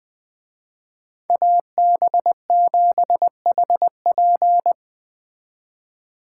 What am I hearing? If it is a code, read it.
Morse code, heard as a keyed tone, about 21 wpm, 700 Hz. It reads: AB7HP